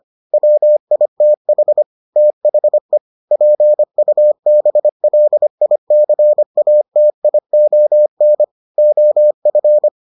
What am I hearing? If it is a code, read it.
Morse code, heard as a keyed tone, about 25 wpm, 600 Hz. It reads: WITH THE PUBLICATION OF